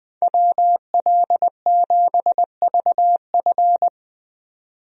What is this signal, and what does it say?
Morse code, heard as a keyed tone, about 20 wpm, 700 Hz. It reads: WL7VF
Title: USA Call Signs - 5 Characters - 20wpm